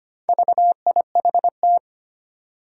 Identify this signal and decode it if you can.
Morse code, heard as a keyed tone, about 25 wpm, 700 Hz. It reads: VIHT
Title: Four Challenging Letter-Number Combinations 25wpm